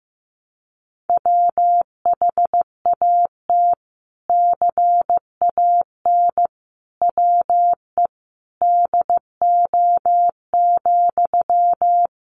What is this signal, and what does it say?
Morse code, heard as a keyed tone, about 15 wpm, 700 Hz. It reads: WHAT CAN WE DO,
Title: Sentences from Top 100 Words 15wpm